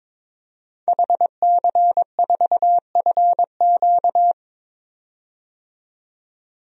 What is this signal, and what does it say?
Morse code, heard as a keyed tone, about 21 wpm, 700 Hz. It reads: HC4FQ
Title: DXCC-100 Call Signs - 5 Characters - 22wpm